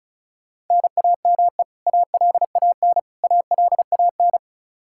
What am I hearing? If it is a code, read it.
Morse code, heard as a keyed tone, about 34 wpm, 700 Hz. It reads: NAME ALAN ALAN
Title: Practice QSO - Exchange 1 - Beginning - 35wpm